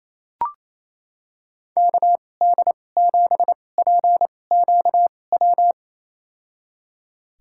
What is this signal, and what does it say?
Morse code, heard as a keyed tone, about 28 wpm, spaced as for 20 wpm, 700 Hz. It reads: KD7PQW
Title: USA Call Signs - 6 Characters - 2x Character Spacing - 28wpm